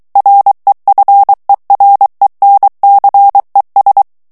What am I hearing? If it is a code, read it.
Morse code, heard as a keyed tone, about 23 wpm, 800 Hz. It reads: REFERENCES
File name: ANGLAIS texte clair 3 vitesse 22 mots minute